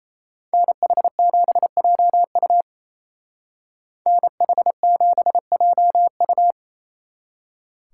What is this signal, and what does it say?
Morse code, heard as a keyed tone, about 36 wpm, 700 Hz. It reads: NH7JU NH7JU